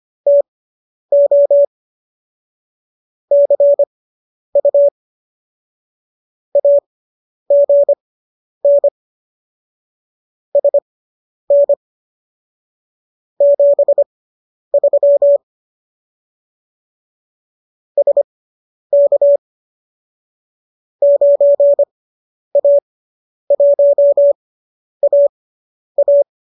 Morse code, heard as a keyed tone, about 25 wpm, spaced as for 10 wpm, 570 Hz. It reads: TO CU AGN SN 73 SK 9A1AA
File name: qso219 10